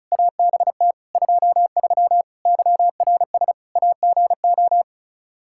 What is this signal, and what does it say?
Morse code, heard as a keyed tone, about 34 wpm, 700 Hz. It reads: ABT 23 YRS AGO
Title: Practice QSO - Exchange 3 - More Info - 35wpm